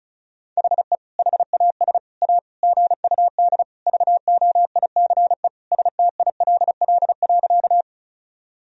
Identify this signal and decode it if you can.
Morse code, heard as a keyed tone, about 34 wpm, 700 Hz. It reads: HE HAS A GUD VOICE STILL.
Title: Sentences from Top 500 Words and Abbreviations 35wpm